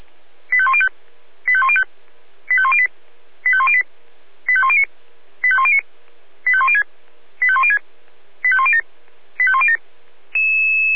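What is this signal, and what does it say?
Five-tone selective call (ZVEI), heard as quick runs of five tones, tones of about 70 ms. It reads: 85105 85105 85108 85108 85109 85109 85106 85106 85107 85107
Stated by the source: ZVEI five-tone alert received by SDR.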